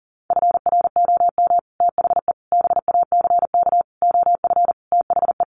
Morse code, heard as a keyed tone, about 40 wpm, 700 Hz. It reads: FROM THE BACK OF THE
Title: Sets of 5 Words Encore 40wpm